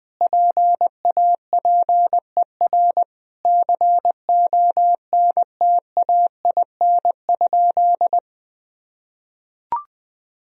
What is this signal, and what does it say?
Morse code, heard as a keyed tone, about 21 wpm, 700 Hz. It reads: PAPER CONTAIN?
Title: Sentences from Top 1000 Words - Review Part 2 - 20wpm